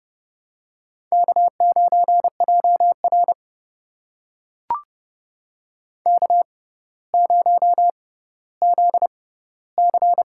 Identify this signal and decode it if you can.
Morse code, heard as a keyed tone, about 30 wpm, spaced as for 15 wpm, 700 Hz. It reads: K9JR K0ZC
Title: USA Call Signs - 4 Characters - 6x Character Spacing - 30wpm